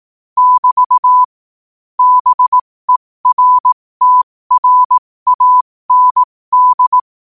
Morse code, heard as a keyed tone, about 18 wpm, spaced as for 15 wpm, 1000 Hz. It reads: <BT> BERTRAND
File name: qotdfiles / 15wpm